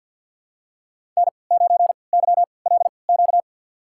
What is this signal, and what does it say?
Morse code, heard as a keyed tone, about 52 wpm, spaced as for 29 wpm, 700 Hz. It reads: N9YLQ
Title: USA Call Signs - 5 Characters - 3x Character Spacing - 50wpm